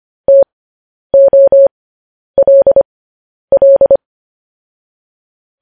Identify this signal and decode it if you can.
Morse code, heard as a keyed tone, about 25 wpm, spaced as for 10 wpm, 550 Hz. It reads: TOLL
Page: news-headlines-in-morse-code-at-10-wpm